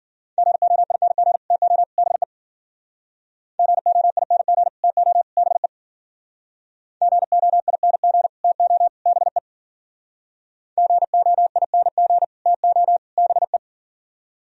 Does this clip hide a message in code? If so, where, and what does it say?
Yes: Morse code from 0.4 to 13.6 s, heard as a keyed tone, about 53 wpm, 700 Hz. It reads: GOING TO BE GOING TO BE GOING TO BE GOING TO BE